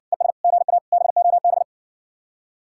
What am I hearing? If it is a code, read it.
Morse code, heard as a keyed tone, about 70 wpm, 700 Hz. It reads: ES GA BOB